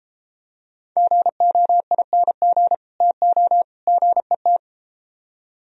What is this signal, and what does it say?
Morse code, heard as a keyed tone, about 32 wpm, 700 Hz. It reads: GOING TO GET